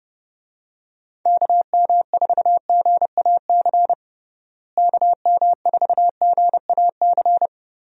Morse code, heard as a keyed tone, about 30 wpm, 700 Hz. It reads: KM4GAC KM4GAC
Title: USA Call Signs - 6 Characters - Repeated 3x - 30wpm